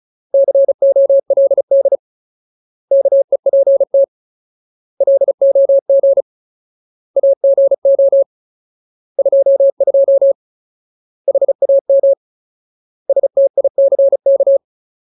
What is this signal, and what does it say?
Morse code, heard as a keyed tone, about 34 wpm, 550 Hz. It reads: COLD KEPT LOG AGO 22 HAM STICK
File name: Ultimate Word List / 35WPM Ultimate